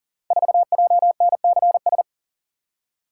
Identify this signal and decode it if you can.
Morse code, heard as a keyed tone, about 42 wpm, 690 Hz. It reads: 4JNCS
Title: Code Groups - 5 Letter:Numbers - 40wpm